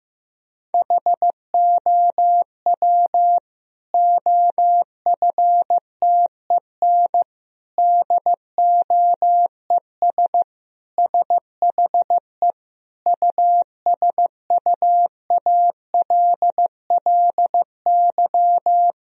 Morse code, heard as a keyed tone, about 15 wpm, 700 Hz. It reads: HOW OFTEN DOES SHE USUALLY
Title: Sentences from Top 1000 Words - Review Part 3 - 15wpm